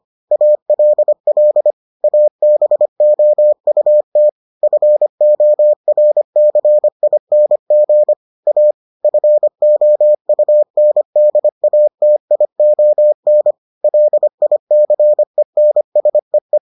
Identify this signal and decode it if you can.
Morse code, heard as a keyed tone, about 25 wpm, 600 Hz. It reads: ALL ABOUT FORCING A FOUNDATION LICENSEE